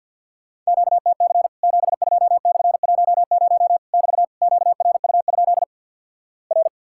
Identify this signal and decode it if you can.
Morse code, heard as a keyed tone, about 50 wpm, 700 Hz. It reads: QTC 72/10 <BT> QRU? R
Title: Worked All Europe DX Contest - QTC Exchange with Repeats - 50wpm